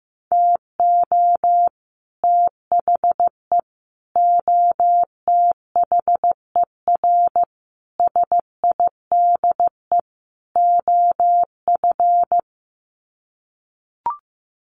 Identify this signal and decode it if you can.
Morse code, heard as a keyed tone, about 15 wpm, 700 Hz. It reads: TO THE OTHER SIDE OF
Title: Sets of 5 Words Encore 15wpm